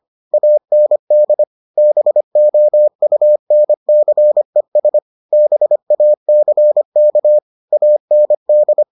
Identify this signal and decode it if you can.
Morse code, heard as a keyed tone, about 25 wpm, 600 Hz. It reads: AND BOUNCES BACK AND